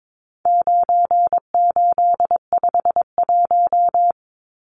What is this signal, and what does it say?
Morse code, heard as a keyed tone, about 22 wpm, 700 Hz. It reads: 9851
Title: Numbers - 4-Digit 22wpm